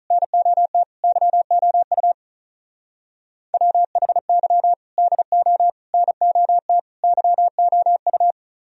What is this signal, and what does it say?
Morse code, heard as a keyed tone, about 42 wpm, 700 Hz. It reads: NOT YOU WHY DO NOT YOU
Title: Speed-Racing - Sets of 4 Words 35wpm